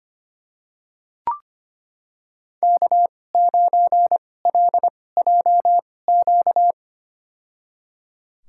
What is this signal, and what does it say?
Morse code, heard as a keyed tone, about 25 wpm, spaced as for 18 wpm, 700 Hz. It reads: K9LJQ
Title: USA Call Signs - 5 Characters - 2x Character Spacing - 25wpm